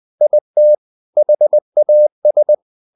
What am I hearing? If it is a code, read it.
Morse code, heard as a keyed tone, about 20 wpm, 600 Hz. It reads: IT HAS
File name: the old man and the sea part 2 20wpm